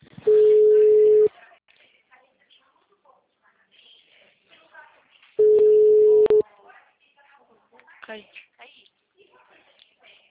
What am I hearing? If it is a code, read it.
Ringback tone (Europe, 425 Hz), with speech.
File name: out- -2015- - - .30807